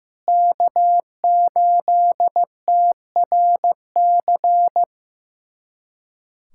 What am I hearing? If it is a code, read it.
Morse code, heard as a keyed tone, about 15 wpm, 700 Hz. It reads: K8TRC
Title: USA Callsign DE Callsign - 5 Characters - 15wpm